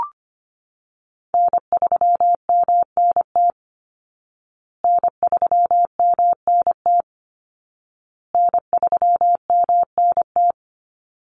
Morse code, heard as a keyed tone, about 25 wpm, 700 Hz. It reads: N3MNT N3MNT N3MNT
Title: Call Signs Repeated 2x - 25wpm